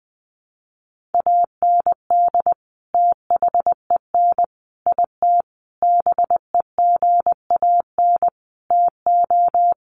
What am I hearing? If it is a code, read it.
Morse code, heard as a keyed tone, about 20 wpm, 700 Hz. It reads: AND THEN IT BEGAN TO
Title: Sentences from Top 1200 Words 20wpm